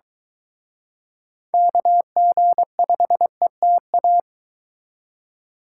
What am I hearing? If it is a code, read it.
Morse code, heard as a keyed tone, about 23 wpm, 700 Hz. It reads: KG5ETA